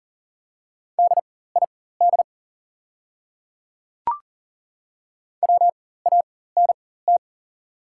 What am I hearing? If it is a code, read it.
Morse code, heard as a keyed tone, about 40 wpm, spaced as for 19 wpm, 700 Hz. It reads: DID WANT
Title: 4x Character Spacing - Top 200 Words - 40wpm